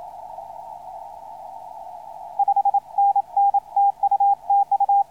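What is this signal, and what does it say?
Morse code, heard as a keyed tone, about 27 wpm, 800 Hz. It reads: 5NN TU TU
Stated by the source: Morse code on the 160 m band, K9W.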